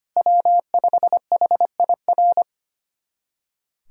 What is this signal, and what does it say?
Morse code, heard as a keyed tone, about 25 wpm, 700 Hz. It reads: W5HIR
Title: USA Callsign DE Callsign 25wpm